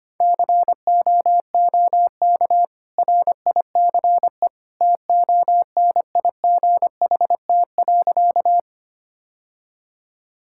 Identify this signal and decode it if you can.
Morse code, heard as a keyed tone, about 25 wpm, 700 Hz. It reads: COOK RICE TONIGHT.
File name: Easy Sentences with Hints - 25wpm